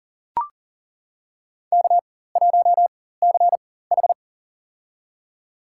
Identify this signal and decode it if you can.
Morse code, heard as a keyed tone, about 40 wpm, spaced as for 19 wpm, 700 Hz. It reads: K1CH
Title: USA Call Signs - 4 Characters - 4x Character Spacing - 40wpm